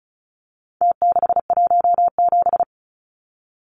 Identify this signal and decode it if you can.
Morse code, heard as a keyed tone, about 34 wpm, 700 Hz. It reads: T617
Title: Numbers - Contest Style - 4-Digit 35wpm